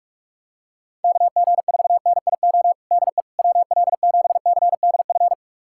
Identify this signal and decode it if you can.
Morse code, heard as a keyed tone, about 46 wpm, 700 Hz. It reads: KG4NIO DE WL7CDF
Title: USA Callsign DE Callsign - 6 Characters - 45wpm